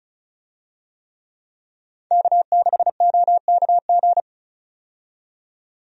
Morse code, heard as a keyed tone, about 36 wpm, 700 Hz. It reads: K6OKG
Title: USA Call Signs - 5 Characters - 35wpm